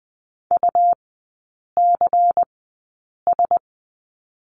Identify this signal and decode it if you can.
Morse code, heard as a keyed tone, about 20 wpm, spaced as for 8 wpm, 700 Hz. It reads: UCS
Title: 178 Lesson 29e - Slash - Three Characters 20wpm